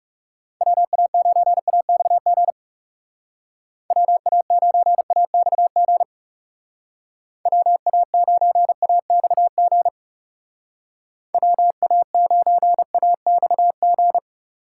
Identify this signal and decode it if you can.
Morse code, heard as a keyed tone, about 38 wpm, 700 Hz. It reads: WA9AXG WA9AXG WA9AXG WA9AXG